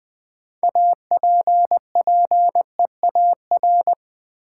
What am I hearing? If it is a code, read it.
Morse code, heard as a keyed tone, about 21 wpm, 700 Hz. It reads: APPEAR
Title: Sentences from Top 1200 Words - New Word Drill Alt - 20wpm